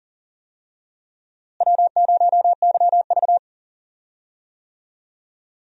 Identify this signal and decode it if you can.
Morse code, heard as a keyed tone, about 40 wpm, 700 Hz. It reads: W0YV
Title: USA Call Signs - 4 Characters - Repeated 2x - 40wpm